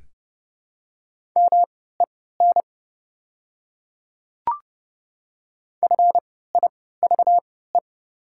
Morse code, heard as a keyed tone, about 30 wpm, spaced as for 17 wpm, 690 Hz. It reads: MEN FIVE
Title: 3x Character Spacing - Top 500 Words - 30wpm